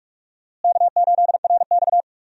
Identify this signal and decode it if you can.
Morse code, heard as a keyed tone, about 43 wpm, 700 Hz. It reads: K8RX
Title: USA Call Signs - 4 Characters - Repeated 2x - 45wpm